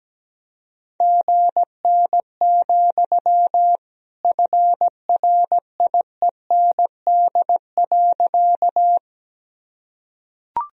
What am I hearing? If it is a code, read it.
Morse code, heard as a keyed tone, about 17 wpm, 700 Hz. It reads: GN, FRIEND.
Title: Sentences from Top 300 Words and Abbreviations 17wpm